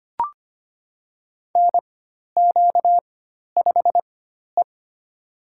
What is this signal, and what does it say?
Morse code, heard as a keyed tone, about 25 wpm, spaced as for 12 wpm, 700 Hz. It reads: NQ5E
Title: USA Call Signs - 4 Characters - 4x Character Spacing - 25wpm